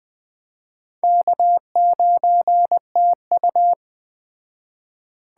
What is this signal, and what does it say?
Morse code, heard as a keyed tone, about 21 wpm, 700 Hz. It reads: K9TU